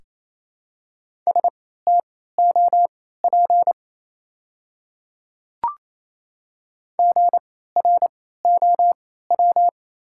Morse code, heard as a keyed tone, about 28 wpm, spaced as for 16 wpm, 700 Hz. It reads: STOP GROW
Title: 3x Character Spacing - Top 300 Words - 28wpm